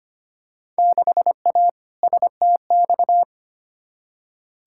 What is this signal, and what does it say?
Morse code, heard as a keyed tone, about 25 wpm, 700 Hz. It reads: 6A STX
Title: Field Day Exchanges - 25wpm